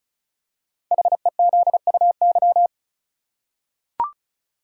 Morse code, heard as a keyed tone, about 36 wpm, 700 Hz. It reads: HEZUY
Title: Code Groups - 5 Letters - 35wpm